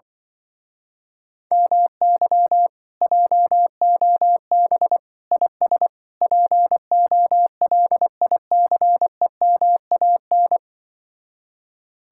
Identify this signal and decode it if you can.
Morse code, heard as a keyed tone, about 24 wpm, 700 Hz. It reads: MY JOB IS POLICEMAN
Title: Speed-Racing Common QSO Phrases Encore - 20wpm